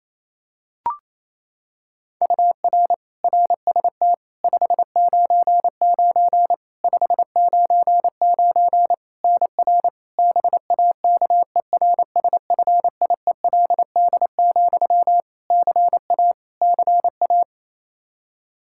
Morse code, heard as a keyed tone, about 28 wpm, 700 Hz. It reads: UR RST 599 599 NR BAKERSFIELD, CA CA
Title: Practice QSO - Exchange 1 - Beginning - 28wpm